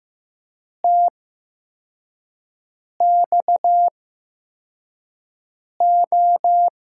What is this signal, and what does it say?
Morse code, heard as a keyed tone, about 15 wpm, spaced as for 4 wpm, 700 Hz. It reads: TXO
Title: USA Call Signs - 5 Characters - 8x Character Spacing - 15wpm